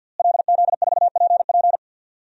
Morse code, heard as a keyed tone, about 52 wpm, 700 Hz. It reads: LZ4PP
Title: DXCC-100 Call Signs - 5 Characters - Repeated 3x - 50wpm